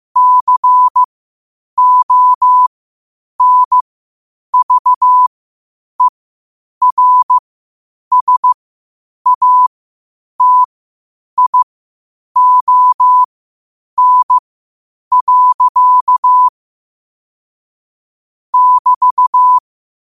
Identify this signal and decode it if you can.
Morse code, heard as a keyed tone, about 15 wpm, spaced as for 8 wpm, 1000 Hz. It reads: CONVERSATION. <BT>
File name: qotdfiles / 8wpm